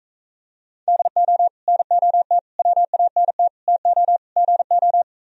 Morse code, heard as a keyed tone, about 45 wpm, 700 Hz. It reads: DO NOT WANT TO GO